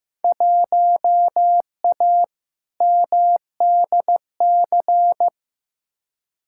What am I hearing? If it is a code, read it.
Morse code, heard as a keyed tone, about 15 wpm, 700 Hz. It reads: 1A MDC
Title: Field Day Exchanges - 15wpm